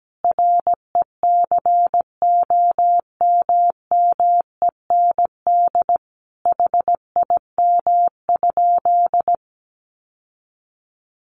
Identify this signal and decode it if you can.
Morse code, heard as a keyed tone, about 17 wpm, 700 Hz. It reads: RECOMMEND HIM?
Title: Sentences from Top 1200 Words 17wpm